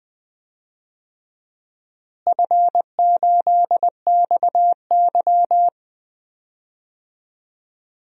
Morse code, heard as a keyed tone, about 20 wpm, 700 Hz. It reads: F8XY
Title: DXCC-100 Call Signs - 4 Characters - Repeated 2x - 20wpm